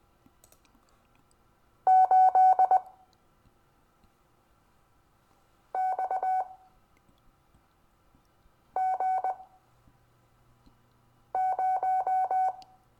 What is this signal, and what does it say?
Morse code, heard as a keyed tone, about 20 wpm, spaced as for 3 wpm, 750 Hz. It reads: 8XG0
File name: cw / Lesson25 6 X